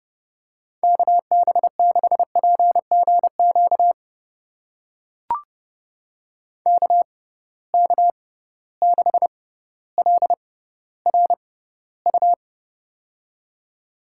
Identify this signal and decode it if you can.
Morse code, heard as a keyed tone, about 30 wpm, spaced as for 15 wpm, 700 Hz. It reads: KB6PGQ KK6LRU
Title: USA Call Signs - 6 Characters - 6x Character Spacing - 30wpm